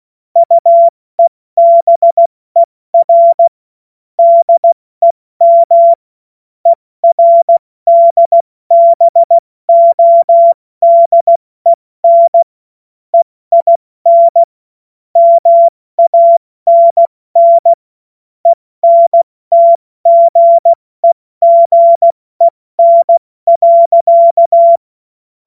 Morse code, heard as a keyed tone, about 16 wpm, 680 Hz. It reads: UEBER DEM ERDBODEN EIN MANN ENTGEGEN.
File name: Morsekurs / Witze / Witz 02. Dezember 2025 70 F80